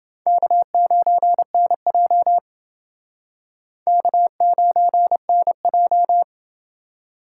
Morse code, heard as a keyed tone, about 29 wpm, 700 Hz. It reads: K9NJ K9NJ